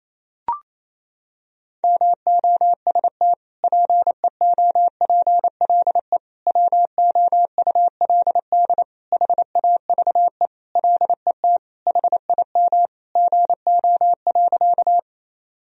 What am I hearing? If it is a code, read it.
Morse code, heard as a keyed tone, about 28 wpm, 700 Hz. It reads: MOST PEOPLE WOULD HAVE LET HIM GO.
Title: Sentences from Top 1000 Words - Review Part 1 - 28wpm